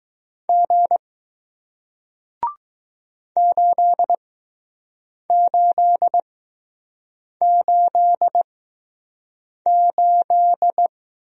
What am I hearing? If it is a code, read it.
Morse code, heard as a keyed tone, about 20 wpm, spaced as for 6 wpm, 700 Hz. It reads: G 8888